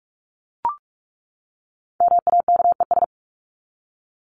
Morse code, heard as a keyed tone, about 44 wpm, 700 Hz. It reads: MAKES
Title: ES:S Suffix - Top 100 Words with Base Words 45wpm